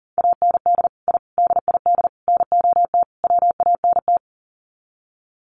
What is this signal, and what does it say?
Morse code, heard as a keyed tone, about 40 wpm, 700 Hz. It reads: AND I DID NOT WANT
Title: Sets of 5 Words 40wpm